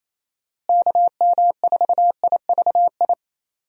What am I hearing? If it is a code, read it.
Morse code, heard as a keyed tone, about 27 wpm, 700 Hz. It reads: KM4IVI
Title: USA Call Signs - 6 Characters - Repeated 2x - 28wpm